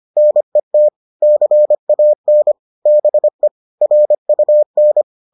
Morse code, heard as a keyed tone, about 25 wpm, 600 Hz. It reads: NET CAN BE RUN